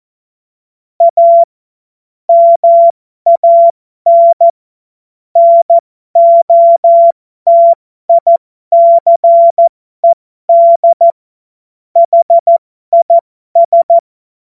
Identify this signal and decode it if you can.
Morse code, heard as a keyed tone, about 14 wpm, spaced as for 12 wpm, 680 Hz. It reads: A MAN NOTICED HIS
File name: Morsekurs / Witze